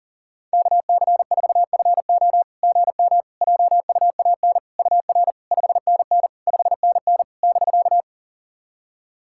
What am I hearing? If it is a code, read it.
Morse code, heard as a keyed tone, about 43 wpm, 690 Hz. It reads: KC4FO GM JUAN UR 5NN 5NN <BK>
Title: SOTA QSO Practice 40wpm